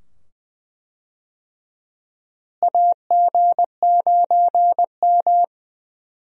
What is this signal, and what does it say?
Morse code, heard as a keyed tone, about 20 wpm, 700 Hz. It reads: AG9M
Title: USA Call Signs - 4 Characters - Repeated 2x - 20wpm